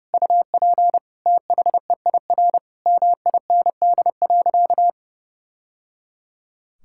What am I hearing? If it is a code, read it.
Morse code, heard as a keyed tone, about 30 wpm, 700 Hz. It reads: UP THEIR MIND.
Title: Sentences from Top 1000 Words - Review Part 5 - 30wpm